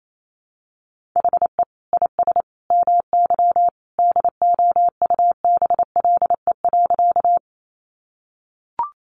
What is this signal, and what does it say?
Morse code, heard as a keyed tone, about 28 wpm, 700 Hz. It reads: HE IS MY DOUBLE.
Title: Sentences from Top 1400 Words 28wpm